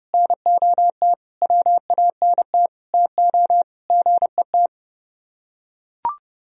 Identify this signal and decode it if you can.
Morse code, heard as a keyed tone, about 30 wpm, 700 Hz. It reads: NOT WANT TO GET